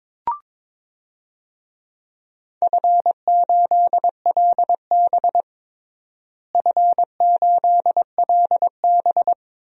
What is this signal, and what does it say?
Morse code, heard as a keyed tone, about 23 wpm, 700 Hz. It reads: F8LB F8LB
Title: DXCC-100 Call Signs - 4 Characters - Repeated 2x - 22wpm